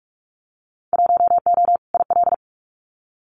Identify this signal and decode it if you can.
Morse code, heard as a keyed tone, about 46 wpm, 700 Hz. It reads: 1O IL
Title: Winter Field Day Exchanges 45wpm